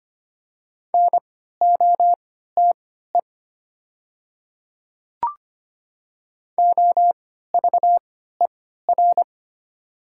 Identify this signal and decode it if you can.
Morse code, heard as a keyed tone, about 25 wpm, spaced as for 14 wpm, 700 Hz. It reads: NOTE OVER
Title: 3x Character Spacing - Top 500 Words - 25wpm